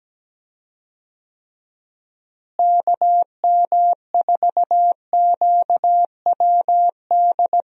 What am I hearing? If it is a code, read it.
Morse code, heard as a keyed tone, about 17 wpm, 700 Hz. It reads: KM4QWD
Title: USA Call Signs - 6 Characters - Repeated 3x - 17wpm